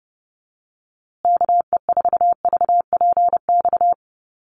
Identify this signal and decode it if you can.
Morse code, heard as a keyed tone, about 30 wpm, 700 Hz. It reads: KE4VPX
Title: USA Call Signs Repeated 2x 30wpm